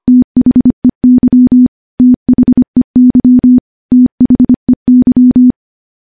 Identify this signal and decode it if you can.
Morse code, heard as a keyed tone, about 25 wpm, 250 Hz. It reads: THEY THEY THEY
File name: morse / morse-training-25WPM